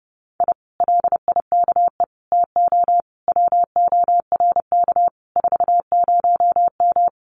Morse code, heard as a keyed tone, about 30 wpm, 700 Hz. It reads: I LIKE TO WORK 40M
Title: Common QSO Phrases Encore 30wpm